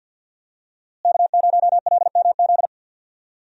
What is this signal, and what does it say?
Morse code, heard as a keyed tone, about 52 wpm, 700 Hz. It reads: K0LMZ
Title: USA Call Signs - 5 Characters - Repeated 3x - 50wpm